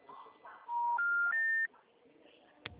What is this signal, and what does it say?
Special information tone.
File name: out- -2011